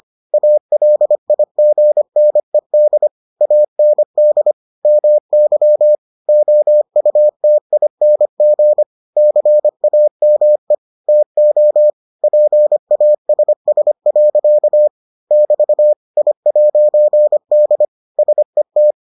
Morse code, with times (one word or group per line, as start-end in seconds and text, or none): 0.3-3.1 ALIGNED
3.4-4.5 AND
4.8-6.0 MY
6.3-8.8 OUTING
9.2-10.7 CAME
11.1-11.9 TO
12.2-14.9 PASS.
15.3-15.9 <BT>
16.2-17.9 I'D
18.2-18.9 SET